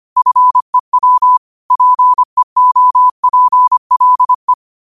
Morse code, heard as a keyed tone, about 25 wpm, 1000 Hz. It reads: FEW PEOPLE
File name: qotdfiles / 25wpm